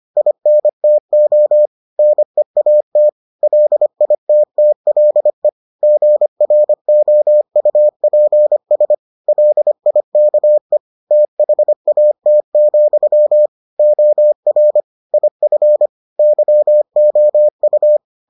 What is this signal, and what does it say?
Morse code, heard as a keyed tone, about 25 wpm, 600 Hz. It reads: INTO NEAT LITTLE GROUPS LIKE THAT, OR IF YOU